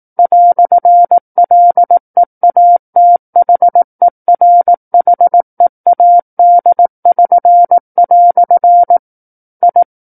Morse code, heard as a keyed tone, about 18 wpm, 700 Hz. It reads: "LEATHERHEAD<SN>" I
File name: 18 WPM